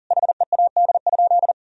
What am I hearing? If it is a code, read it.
Morse code, heard as a keyed tone, about 40 wpm, 700 Hz. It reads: HEAD?
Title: Sentences from Top 1000 Words - Review Part 3 - 40wpm